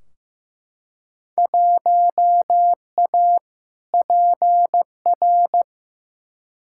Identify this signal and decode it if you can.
Morse code, heard as a keyed tone, about 15 wpm, 700 Hz. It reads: 1A PR
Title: Field Day Exchanges Encore - 15wpm